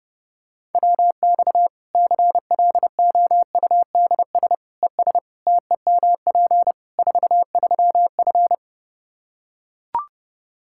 Morse code, heard as a keyed tone, about 32 wpm, 700 Hz. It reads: WX CLOUDS ES TEMP 43F
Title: K1USN SST QSO Practice 30wpm